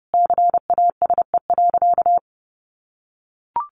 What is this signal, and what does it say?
Morse code, heard as a keyed tone, about 30 wpm, 700 Hz. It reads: CASE.
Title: Sentences from Top 1000 Words 30wpm